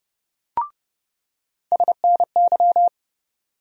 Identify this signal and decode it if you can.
Morse code, heard as a keyed tone, about 31 wpm, 700 Hz. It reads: SNY
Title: USA Call Sign Postfixes - 3 Letters 30wpm